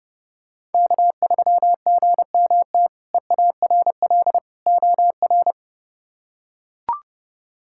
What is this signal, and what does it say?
Morse code, heard as a keyed tone, about 32 wpm, 700 Hz. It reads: K3GMT EARL OR
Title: K1USN SST QSO Practice - 30wpm